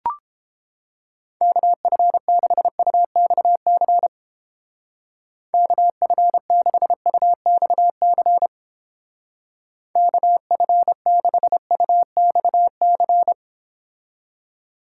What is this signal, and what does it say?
Morse code, heard as a keyed tone, about 29 wpm, 700 Hz. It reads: KF6UXC KF6UXC KF6UXC